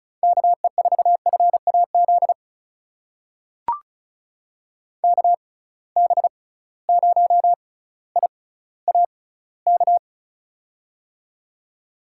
Morse code, heard as a keyed tone, about 36 wpm, spaced as for 18 wpm, 700 Hz. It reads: KE4FAZ KB0IAK
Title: USA Call Signs - 6 Characters - 6x Character Spacing - 35wpm